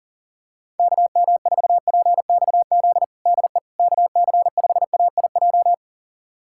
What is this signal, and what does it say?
Morse code, heard as a keyed tone, about 40 wpm, 700 Hz. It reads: KM4PXZ DE KC5AIJ
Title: USA Callsign DE Callsign - 6 Characters - 40wpm